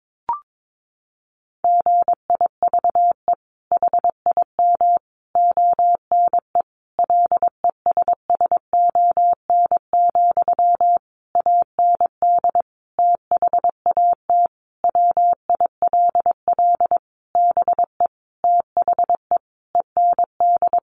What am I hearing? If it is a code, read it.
Morse code, heard as a keyed tone, about 22 wpm, 700 Hz. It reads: GIVE HIM ONE LESSON, AND THAT WILL BE THE END
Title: Sentences from Top 1200 Words 22wpm